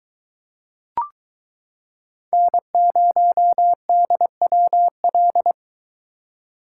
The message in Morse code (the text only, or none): N0DWL